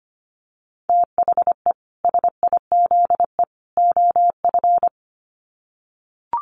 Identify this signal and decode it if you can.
Morse code, heard as a keyed tone, about 25 wpm, 700 Hz. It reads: THE SIZE OF
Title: Sets of 3 Words 25wpm